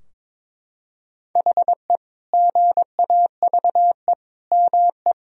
Morse code, heard as a keyed tone, about 22 wpm, 700 Hz. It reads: HE GAVE ME
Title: Sentences from Top 1000 Words - Review Part 4 - 22wpm